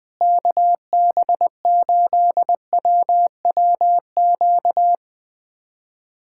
Morse code, heard as a keyed tone, about 20 wpm, 700 Hz. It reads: KB8WWQ
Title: USA Callsign DE Callsign - 6 Characters - 20wpm